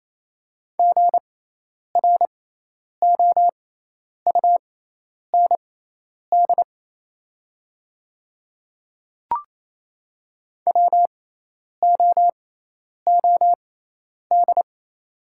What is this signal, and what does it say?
Morse code, heard as a keyed tone, about 28 wpm, spaced as for 10 wpm, 700 Hz. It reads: GROUND WOOD
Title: 6x Character Spacing - Top 500 Words - 28wpm